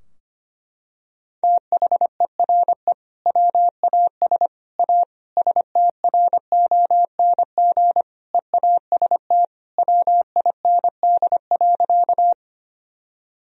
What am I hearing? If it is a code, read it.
Morse code, heard as a keyed tone, about 25 wpm, 700 Hz. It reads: THERE WAS A STRONG EAST WIND.
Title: Sentences from Top 1000 Words - Review Part 3 - 25wpm